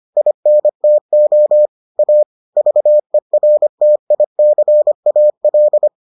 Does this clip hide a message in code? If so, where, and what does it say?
Yes: Morse code from 0.2 to 5.9 s, heard as a keyed tone, about 25 wpm, 600 Hz. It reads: INTO A VERTICAL